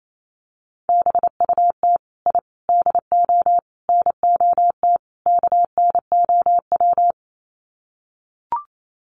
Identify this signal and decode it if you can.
Morse code, heard as a keyed tone, about 28 wpm, 700 Hz. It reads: BUT I DO NOT KNOW
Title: Sets of 5 Words 28wpm